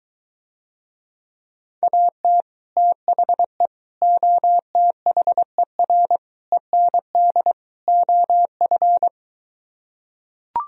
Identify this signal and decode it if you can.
Morse code, heard as a keyed tone, about 23 wpm, 700 Hz. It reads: AT THE OTHER END OF